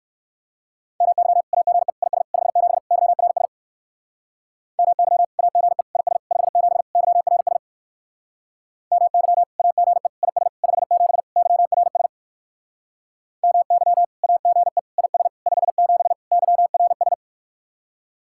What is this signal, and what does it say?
Morse code, heard as a keyed tone, about 58 wpm, 700 Hz. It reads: MY AGE IS 57 YRS MY AGE IS 57 YRS MY AGE IS 57 YRS MY AGE IS 57 YRS